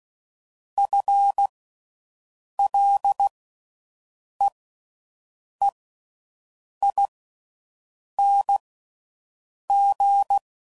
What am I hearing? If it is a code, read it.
Morse code, heard as a keyed tone, about 16 wpm, spaced as for 6 wpm, 800 Hz. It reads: FLEEING